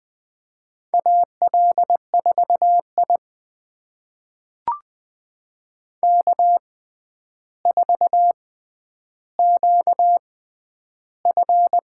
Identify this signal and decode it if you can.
Morse code, heard as a keyed tone, about 20 wpm, spaced as for 10 wpm, 700 Hz. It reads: AL4I K4QF
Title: USA Call Signs - 4 Characters - 6x Character Spacing - 20wpm